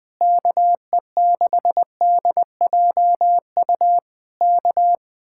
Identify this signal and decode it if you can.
Morse code, heard as a keyed tone, about 20 wpm, 700 Hz. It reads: KE6DJU K
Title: Speed-Racing Common QSO Phrases - 17wpm